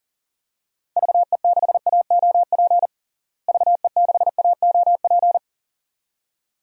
Morse code, heard as a keyed tone, about 40 wpm, 700 Hz. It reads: VE6AOP VE6AOP
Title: DXCC-100 Call Signs - 6 Characters - Repeated 2x - 40wpm